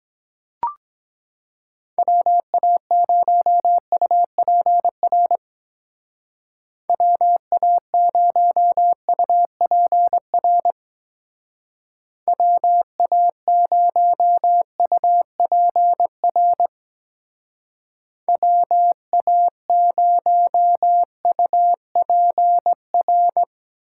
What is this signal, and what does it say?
Morse code, heard as a keyed tone, about 21 wpm, 700 Hz. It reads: WA0UPR WA0UPR WA0UPR WA0UPR
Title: Speed Racing - USA Call Signs 17wpm